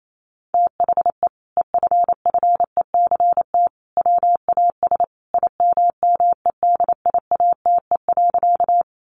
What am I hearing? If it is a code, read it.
Morse code, heard as a keyed tone, about 28 wpm, 700 Hz. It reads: THE EFFECT WAS IMMEDIATE.
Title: Sentences from Top 1200 Words 28wpm